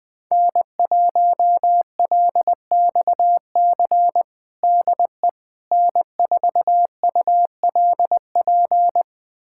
Morse code, heard as a keyed tone, about 20 wpm, 700 Hz. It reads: N1LXC DE N4ULP
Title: USA Callsign DE Callsign - 5 Characters - 20wpm